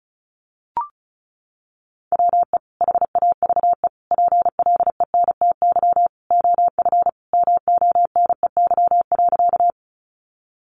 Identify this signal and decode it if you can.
Morse code, heard as a keyed tone, about 34 wpm, 700 Hz. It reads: WE HAVE PLENTY OF MONEY.
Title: Sentences from Top 1300 Words 35wpm